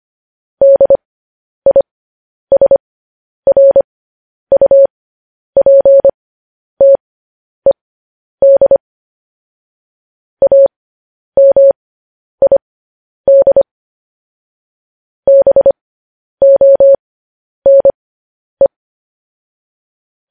Morse code, heard as a keyed tone, about 25 wpm, spaced as for 10 wpm, 550 Hz. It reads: DISRUPTED AMID BONE
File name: podcast / News headlines-1x-25x10